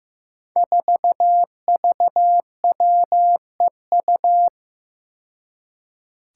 Morse code, heard as a keyed tone, about 15 wpm, 700 Hz. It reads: 4VWEU
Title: Five Challenging Letter-Number Combinations 15wpm